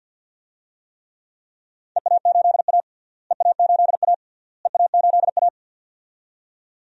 Morse code, heard as a keyed tone, about 47 wpm, 700 Hz. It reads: EA8A EA8A EA8A
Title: DXCC-100 Call Signs - 4 Characters - Repeated 3x - 50wpm